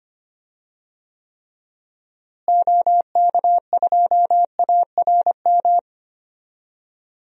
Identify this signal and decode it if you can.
Morse code, heard as a keyed tone, about 25 wpm, 700 Hz. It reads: OK2ARM
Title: DXCC-100 Call Signs - 6 Characters - Repeated 3x - 25wpm